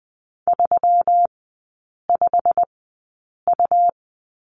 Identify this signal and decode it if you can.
Morse code, heard as a keyed tone, about 20 wpm, spaced as for 8 wpm, 700 Hz. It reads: 35U
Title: 145 Lesson 24e - Letter y - Three Characters 20wpm